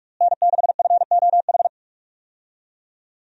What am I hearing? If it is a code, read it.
Morse code, heard as a keyed tone, about 46 wpm, 700 Hz. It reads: N6FOH
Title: USA Call Signs - 5 Characters - Repeated 3x - 45wpm